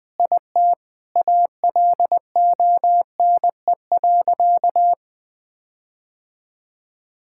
Morse code, heard as a keyed tone, about 21 wpm, 700 Hz. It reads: IT ALONE.
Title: Sentences with 7-Word Phrases - 20wpm